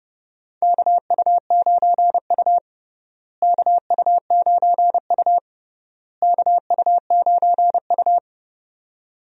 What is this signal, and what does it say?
Morse code, heard as a keyed tone, about 32 wpm, 700 Hz. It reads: KU9U KU9U KU9U
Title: USA Call Signs - 4 Characters - Repeated 3x - 30wpm